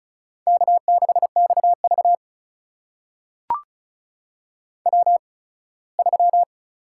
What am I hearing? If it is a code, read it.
Morse code, heard as a keyed tone, about 36 wpm, spaced as for 21 wpm, 700 Hz. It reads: K6XV W3
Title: USA Call Signs - 4 Characters - 8x Character Spacing - 35wpm